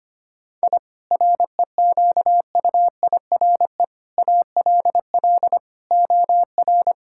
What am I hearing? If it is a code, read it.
Morse code, heard as a keyed tone, about 25 wpm, 700 Hz. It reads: I REQUIRE ALL OR
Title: Sentences from Top 1000 Words - Review Part 6 - 25wpm